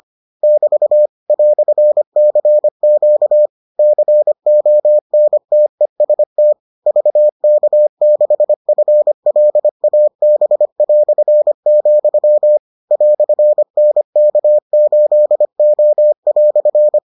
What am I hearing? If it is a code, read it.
Morse code, heard as a keyed tone, about 25 wpm, 600 Hz. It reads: <BT> "CQ CONTEST VK6FLAB", "NK8O"